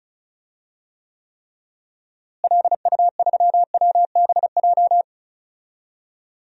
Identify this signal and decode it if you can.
Morse code, heard as a keyed tone, about 36 wpm, 700 Hz. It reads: LU3WBJ
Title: DXCC-100 Call Signs - 6 Characters - Repeated 3x - 35wpm